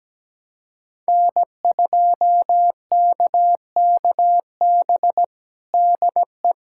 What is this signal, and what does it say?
Morse code, heard as a keyed tone, about 17 wpm, 700 Hz. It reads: N2KKB DE
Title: USA Callsign DE Callsign 17wpm